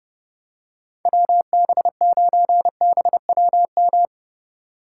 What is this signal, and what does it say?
Morse code, heard as a keyed tone, about 30 wpm, 700 Hz. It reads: WB9BWM
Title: USA Call Signs - 4x Character Spacing - 30wpm